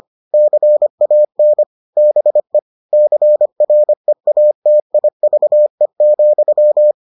Morse code, heard as a keyed tone, about 25 wpm, 600 Hz. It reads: CAN BE CREATIVE,